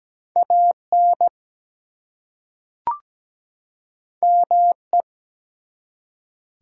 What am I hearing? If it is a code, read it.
Morse code, heard as a keyed tone, about 17 wpm, 700 Hz. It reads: AN ME
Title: Two Letter Words - 17wpm